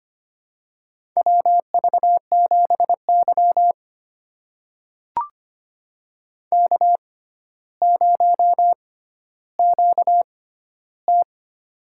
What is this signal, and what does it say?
Morse code, heard as a keyed tone, about 25 wpm, spaced as for 13 wpm, 700 Hz. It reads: WV7Y K0QT
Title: USA Call Signs - 4 Characters - 6x Character Spacing - 25wpm